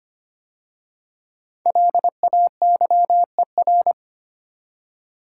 Morse code, heard as a keyed tone, about 25 wpm, 700 Hz. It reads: LAYER
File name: 5 Letter Words - First 3 Letters Given - 25wpm